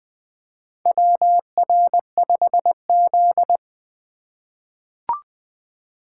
Morse code, heard as a keyed tone, about 20 wpm, 700 Hz. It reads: WR5Z
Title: USA Call Signs - 4 Characters - 3x Character Spacing - 20wpm